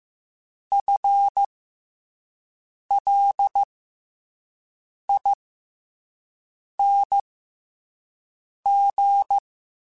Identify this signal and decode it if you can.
Morse code, heard as a keyed tone, about 15 wpm, spaced as for 5 wpm, 800 Hz. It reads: FLING
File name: FL / FL mp3s / FL Prefixes 3x2 @5